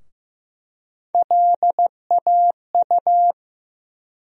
Morse code, heard as a keyed tone, about 15 wpm, 700 Hz. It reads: LAU
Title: USA Call Sign Postfixes - 3 Letters 15wpm